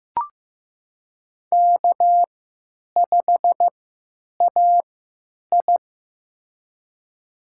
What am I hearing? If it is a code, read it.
Morse code, heard as a keyed tone, about 15 wpm, spaced as for 9 wpm, 700 Hz. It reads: K5AI
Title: USA Call Signs - 4 Characters - 3x Character Spacing - 15wpm